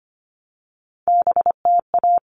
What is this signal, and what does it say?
Morse code, heard as a keyed tone, about 25 wpm, 700 Hz. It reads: BTA
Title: All Cut Numbers - 3-Digit 25wpm